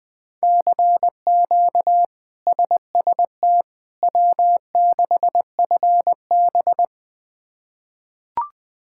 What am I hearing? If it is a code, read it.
Morse code, heard as a keyed tone, about 20 wpm, 700 Hz. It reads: CQ SST W6FB
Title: K1USN SST QSO Practice - 20wpm